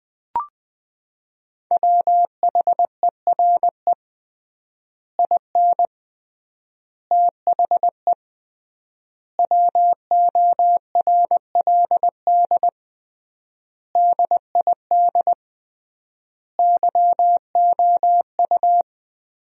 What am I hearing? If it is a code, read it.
Morse code, heard as a keyed tone, about 20 wpm, 700 Hz. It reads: WHERE IN THE WORLD DID YOU
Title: 3x Word-Spacing - Sentences from Top 200 Words - 20wpm